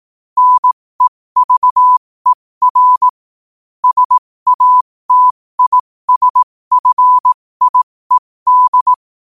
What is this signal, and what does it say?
Morse code, heard as a keyed tone, about 18 wpm, spaced as for 15 wpm, 1000 Hz. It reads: NEVER SATISFIED
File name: qotdfiles / 15wpm